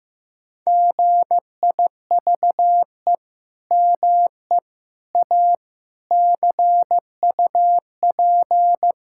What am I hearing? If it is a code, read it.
Morse code, heard as a keyed tone, about 15 wpm, 700 Hz. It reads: GIVE ME A CUP
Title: Sentences from Top 1600 Words 15wpm